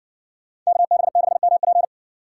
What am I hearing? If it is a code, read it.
Morse code, heard as a keyed tone, about 67 wpm, 700 Hz. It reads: KB6MP